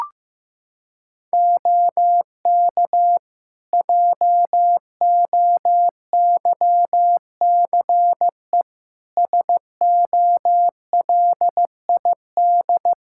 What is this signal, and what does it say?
Morse code, heard as a keyed tone, about 15 wpm, 700 Hz. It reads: OK JOYCE SOLID
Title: Practice QSO - Exchange 3 - More Info - 15wpm